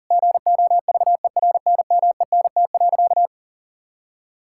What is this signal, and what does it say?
Morse code, heard as a keyed tone, about 37 wpm, 700 Hz. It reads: GOVERNMENT.
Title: Sentences from Top 1000 Words - Review Part 5 - 40wpm